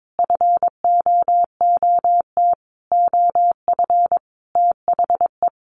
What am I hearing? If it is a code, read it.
Morse code, heard as a keyed tone, about 22 wpm, 700 Hz. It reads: FOOT OF THE
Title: Sentences from Top 1400 Words 22wpm